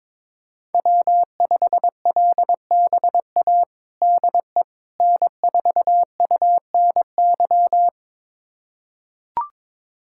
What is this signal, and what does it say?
Morse code, heard as a keyed tone, about 21 wpm, 700 Hz. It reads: W5LBA DE N4UNY
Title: USA Callsign DE Callsign - 5 Characters - 22wpm